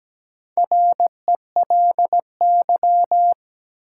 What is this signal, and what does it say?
Morse code, heard as a keyed tone, about 17 wpm, 700 Hz. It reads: RELY
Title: Sentences from Top 1200 Words - New Word Drill Alt - 17wpm